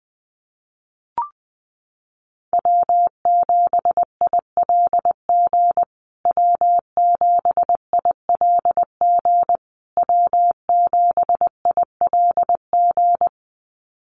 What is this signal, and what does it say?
Morse code, heard as a keyed tone, about 20 wpm, 700 Hz. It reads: W7ILG W7ILG W7ILG
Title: USA Call Signs Repeated 3x 20wpm